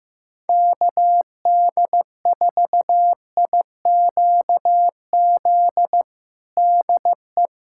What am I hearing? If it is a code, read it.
Morse code, heard as a keyed tone, about 15 wpm, 700 Hz. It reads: KD4IQZ DE
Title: USA Callsign DE Callsign - 6 Characters - 15wpm